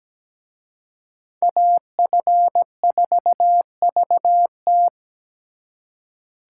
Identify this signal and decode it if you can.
Morse code, heard as a keyed tone, about 20 wpm, 700 Hz. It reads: AF4VT
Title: Speed Racing - USA 2x2 Call Signs 17wpm